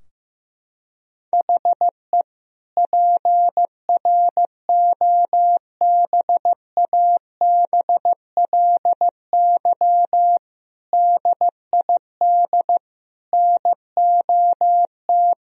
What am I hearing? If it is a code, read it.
Morse code, heard as a keyed tone, about 15 wpm, 700 Hz. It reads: HE PROBABLY DID NOT
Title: Sentences from Top 1000 Words - Review Part 4 - 15wpm